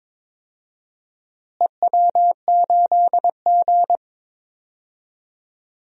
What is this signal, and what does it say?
Morse code, heard as a keyed tone, about 22 wpm, 700 Hz. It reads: EW8G
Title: DXCC-100 Call Signs - 4 Characters - Repeated 3x - 22wpm